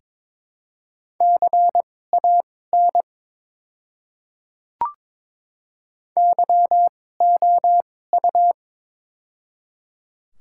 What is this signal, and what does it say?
Morse code, heard as a keyed tone, about 22 wpm, spaced as for 16 wpm, 700 Hz. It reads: CAN YOU
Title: 2x Character Spacing - Top 100 Words - 22wpm